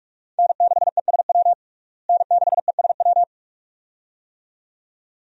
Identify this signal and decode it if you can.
Morse code, heard as a keyed tone, about 46 wpm, 700 Hz. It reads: N6ESW N6ESW
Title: USA Call Signs - 5 Characters - Repeated 2x - 45wpm